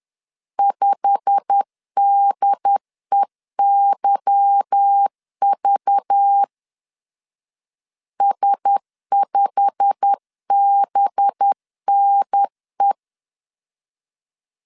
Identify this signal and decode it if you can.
Morse code, heard as a keyed tone, about 10 wpm, 800 Hz. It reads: 5DEYV S5BNE
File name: LEZIONI / 8 YJ45 / 8 YJ45 10wpm 2549